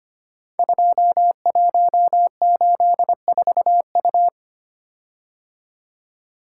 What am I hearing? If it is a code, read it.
Morse code, heard as a keyed tone, about 25 wpm, 700 Hz. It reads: 2184U
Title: Code Groups - 5 Letter:Numbers - 25wpm